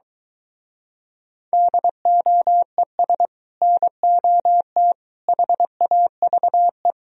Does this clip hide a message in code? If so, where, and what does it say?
Yes: Morse code from 1.5 to 6.9 s, heard as a keyed tone, about 23 wpm, 700 Hz. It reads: DOES NOT HAVE